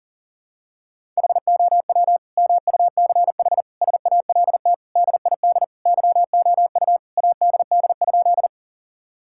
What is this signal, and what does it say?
Morse code, heard as a keyed tone, about 40 wpm, 700 Hz. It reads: HOW MUCH SALT DID YOU ADD?
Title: Sentences from Top 1000 Words - Review Part 5 - 40wpm